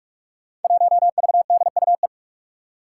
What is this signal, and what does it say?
Morse code, heard as a keyed tone, about 43 wpm, 700 Hz. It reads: 1VDUE
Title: Five Challenging Letter-Number Combinations 45wpm